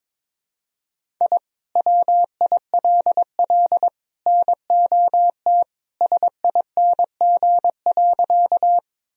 Morse code, heard as a keyed tone, about 22 wpm, 700 Hz. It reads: I WILL NOT SING.
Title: Sentences from Top 1000 Words - Review Part 2 - 22wpm